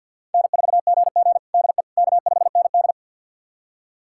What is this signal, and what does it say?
Morse code, heard as a keyed tone, about 46 wpm, 700 Hz. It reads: N4GG DE K5ND
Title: USA Callsign DE Callsign - 4 Characters - 50wpm